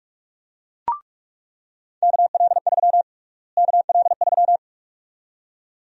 Morse code, heard as a keyed tone, about 46 wpm, 700 Hz. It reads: KL3 KL3
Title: USA Call Sign Prefixes - Repeated 2x - 45wpm